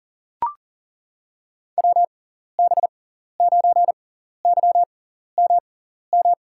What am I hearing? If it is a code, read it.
Morse code, heard as a keyed tone, about 40 wpm, spaced as for 14 wpm, 700 Hz. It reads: WB9YMM
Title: USA Call Signs - 6x Character Spacing - 40wpm